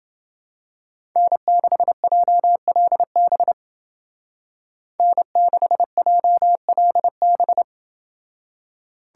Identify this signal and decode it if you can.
Morse code, heard as a keyed tone, about 29 wpm, 700 Hz. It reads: N6JLB N6JLB